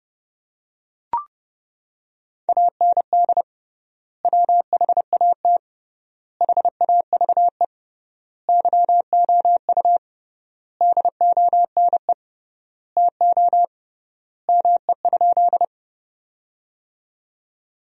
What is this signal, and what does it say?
Morse code, heard as a keyed tone, about 30 wpm, 700 Hz. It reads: AND WHAT HAVE YOU DONE TO ME?
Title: 3x Word-Spacing - Sentences from Top 500 Words - 30wpm